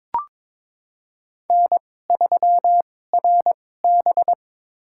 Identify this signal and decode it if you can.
Morse code, heard as a keyed tone, about 22 wpm, spaced as for 16 wpm, 700 Hz. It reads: N3RB
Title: USA Call Signs - 4 Characters - 2x Character Spacing - 22wpm